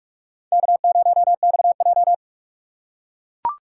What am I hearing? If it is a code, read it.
Morse code, heard as a keyed tone, about 40 wpm, 700 Hz. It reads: K0XJ